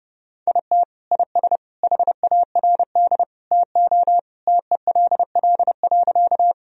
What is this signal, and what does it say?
Morse code, heard as a keyed tone, about 32 wpm, 700 Hz. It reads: IT IS HARD TO TELL.
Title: Sentences from Top 1000 Words - Review Part 1 - 30wpm